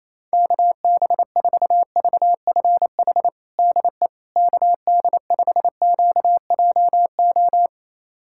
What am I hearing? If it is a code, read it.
Morse code, heard as a keyed tone, about 28 wpm, 700 Hz. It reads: KB4VFH DE KD5QJO
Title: USA Callsign DE Callsign 28wpm